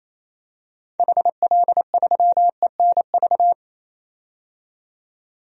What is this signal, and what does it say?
Morse code, heard as a keyed tone, about 28 wpm, 700 Hz. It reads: HL3ENV
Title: DXCC-100 Call Signs - 6 Characters - 28wpm